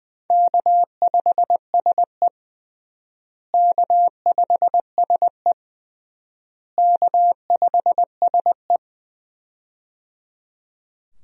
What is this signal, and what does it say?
Morse code, heard as a keyed tone, about 20 wpm, 700 Hz. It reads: K5SE K5SE K5SE
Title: USA Call Signs - 4 Characters - Repeated 3x - 20wpm